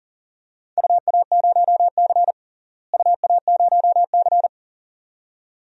Morse code, heard as a keyed tone, about 40 wpm, 700 Hz. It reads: UA0C UA0C
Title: DXCC-100 Call Signs - 4 Characters - Repeated 2x - 40wpm